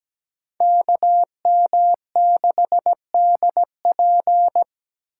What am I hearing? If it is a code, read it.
Morse code, heard as a keyed tone, about 20 wpm, 700 Hz. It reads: KM6DP
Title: Speed Racing - USA 2x2 Call Signs 17wpm